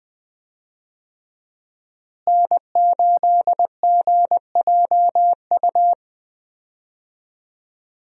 Morse code, heard as a keyed tone, about 20 wpm, 700 Hz. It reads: N8GJU
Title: USA Call Signs - 5 Characters - Repeated 2x - 20wpm